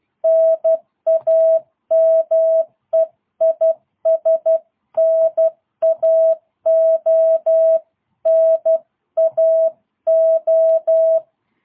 Morse code, heard as a keyed tone, about 12 wpm, 650 Hz. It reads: NAME IS NAO NAO
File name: jh8cbh / 2022cw